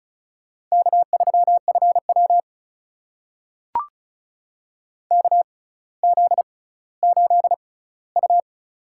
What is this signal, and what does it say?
Morse code, heard as a keyed tone, about 34 wpm, spaced as for 18 wpm, 700 Hz. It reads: K3FW KZ8U
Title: USA Call Signs - 4 Characters - 6x Character Spacing - 35wpm